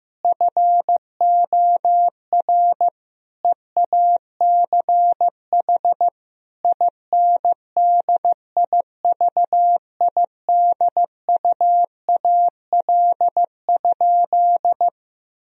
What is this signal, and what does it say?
Morse code, heard as a keyed tone, about 15 wpm, 700 Hz. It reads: FOR EACH INDIVIDUAL?
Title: Sentences from Top 1000 Words - Review Part 4 - 15wpm